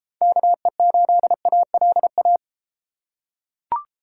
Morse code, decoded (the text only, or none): KE8ALA